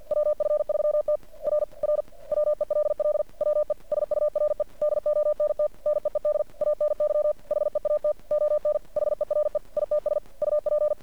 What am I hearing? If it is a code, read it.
Morse code, heard as a keyed tone, about 51 wpm, 600 Hz. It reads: WF3T R R WELL WE SURE DONT NEED ANY HEAT ON HERE ITS UP
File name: archived / CRSnet